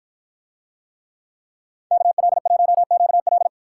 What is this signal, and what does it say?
Morse code, heard as a keyed tone, about 53 wpm, 700 Hz. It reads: KL1QL